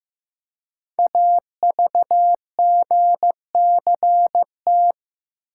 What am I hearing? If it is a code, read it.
Morse code, heard as a keyed tone, about 16 wpm, 700 Hz. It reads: AVGCT
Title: Code Groups - 5 Letter:Numbers - 15wpm